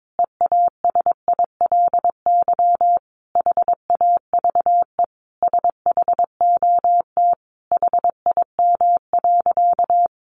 Morse code, heard as a keyed tone, about 22 wpm, 700 Hz. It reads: EASILY HAVE SHOT HIM.
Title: Sentences from Top 1100 Words 22wpm